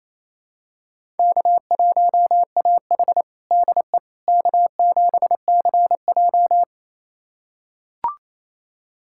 Morse code, heard as a keyed tone, about 28 wpm, 700 Hz. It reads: K1AH DE K7CJ
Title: USA Callsign DE Callsign - 4 Characters - 28wpm